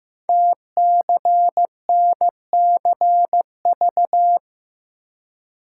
Morse code, heard as a keyed tone, about 15 wpm, 700 Hz. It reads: TCNCV
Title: Code Groups - 5 Letters - 15wpm